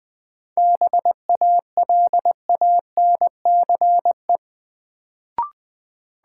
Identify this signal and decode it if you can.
Morse code, heard as a keyed tone, about 20 wpm, 700 Hz. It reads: BALANCE
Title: Sentences from Top 900 Words - New Word Drill - 20wpm